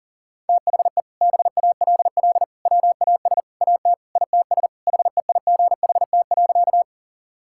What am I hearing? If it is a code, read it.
Morse code, heard as a keyed tone, about 37 wpm, 700 Hz. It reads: THE BALL WAS AT ITS HEIGHT.
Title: Sentences from Top 1600 Words 40wpm